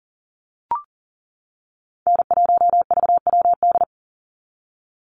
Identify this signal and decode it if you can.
Morse code, heard as a keyed tone, about 38 wpm, 700 Hz. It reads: N1VWD
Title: NA SOTA Chaser Callsigns 40wpm